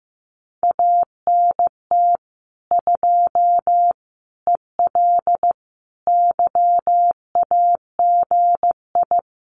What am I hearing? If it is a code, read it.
Morse code, heard as a keyed tone, about 15 wpm, 700 Hz. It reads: ANT 2 EL YAGI
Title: Common QSO Phrases Encore 15wpm